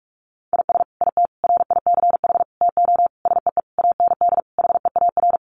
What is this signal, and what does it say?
Morse code, heard as a keyed tone, about 44 wpm, 700 Hz. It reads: IS IT RICH TO SEE AND HEAR
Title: Sentences from Top 1100 Words 45wpm